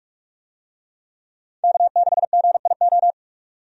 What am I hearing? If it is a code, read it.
Morse code, heard as a keyed tone, about 46 wpm, 700 Hz. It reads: K6GIO
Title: USA Call Signs - 5 Characters - 45wpm